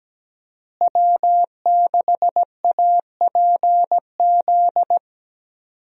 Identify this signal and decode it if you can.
Morse code, heard as a keyed tone, about 17 wpm, 700 Hz. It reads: W6APZ
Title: USA Call Signs - 5 Characters - Repeated 3x - 17wpm